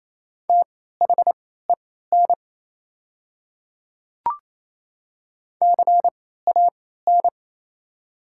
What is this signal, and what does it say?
Morse code, heard as a keyed tone, about 28 wpm, spaced as for 16 wpm, 700 Hz. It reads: THEN CAN
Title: 3x Character Spacing - Top 100 Words - 28wpm